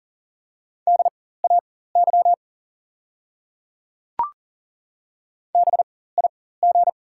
Morse code, heard as a keyed tone, about 39 wpm, spaced as for 19 wpm, 690 Hz. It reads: DAY BIG
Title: 4x Character Spacing - Top 200 Words - 40wpm